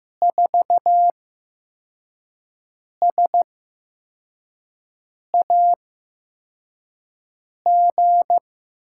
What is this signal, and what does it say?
Morse code, heard as a keyed tone, about 15 wpm, spaced as for 4 wpm, 700 Hz. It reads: 4SAG
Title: USA Call Signs - 5 Characters - 8x Character Spacing - 15wpm